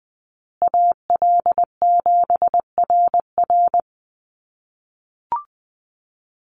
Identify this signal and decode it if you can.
Morse code, heard as a keyed tone, about 20 wpm, 700 Hz. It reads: AL7RR
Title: USA Call Signs Repeated 3x 20wpm